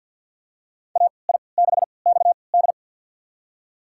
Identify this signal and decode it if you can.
Morse code, heard as a keyed tone, about 47 wpm, spaced as for 28 wpm, 700 Hz. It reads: AI6XD
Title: USA Call Signs - 5 Characters - 3x Character Spacing - 50wpm